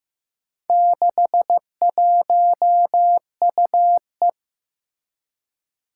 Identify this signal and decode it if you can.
Morse code, heard as a keyed tone, about 15 wpm, 700 Hz. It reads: 61UE
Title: Four Challenging Letter-Number Combinations 15wpm